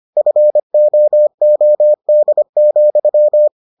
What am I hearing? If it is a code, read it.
Morse code, heard as a keyed tone, about 25 wpm, 600 Hz. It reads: FOOD,